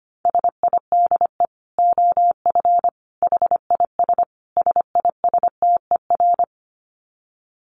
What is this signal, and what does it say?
Morse code, heard as a keyed tone, about 25 wpm, 700 Hz. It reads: SIDE OF HIS SISTER
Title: Dit Challenge 25wpm